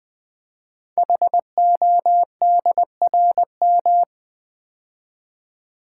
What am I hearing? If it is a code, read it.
Morse code, heard as a keyed tone, about 20 wpm, 700 Hz. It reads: HODRM
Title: Code Groups - 5 Letters - 20wpm